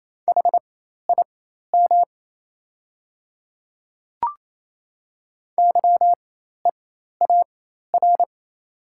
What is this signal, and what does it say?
Morse code, heard as a keyed tone, about 27 wpm, spaced as for 13 wpm, 690 Hz. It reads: HIM YEAR
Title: 4x Character Spacing - Top 200 Words - 28wpm